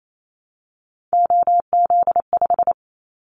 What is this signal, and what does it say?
Morse code, heard as a keyed tone, about 28 wpm, 700 Hz. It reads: OZ5
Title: Callsign Prefixes for Top 50 DXCC - 28wpm